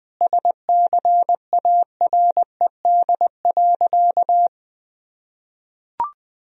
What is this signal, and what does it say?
Morse code, heard as a keyed tone, about 20 wpm, 700 Hz. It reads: SCARED.
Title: Sentences from Top 1000 Words - Review Part 4 - 20wpm